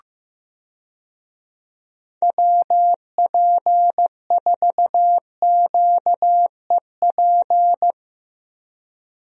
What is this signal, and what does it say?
Morse code, heard as a keyed tone, about 15 wpm, 700 Hz. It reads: WP4QEP
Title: USA Call Signs - 6 Characters - Repeated 3x - 15wpm